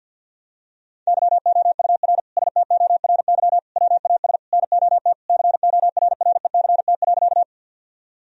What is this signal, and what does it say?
Morse code, heard as a keyed tone, about 47 wpm, 700 Hz. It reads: YOUR STORY WAS NOT CORRECT.
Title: Sentences from Top 1000 Words - Review Part 2 - 50wpm